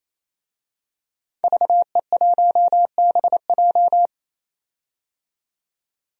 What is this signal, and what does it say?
Morse code, heard as a keyed tone, about 28 wpm, 700 Hz. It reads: VE1BJ
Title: DXCC-100 Call Signs - 5 Characters - 28wpm